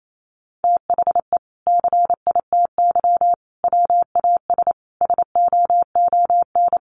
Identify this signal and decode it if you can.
Morse code, heard as a keyed tone, about 28 wpm, 700 Hz. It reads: THE CITY WAS SOON
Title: Sentences from Top 500 Words 28wpm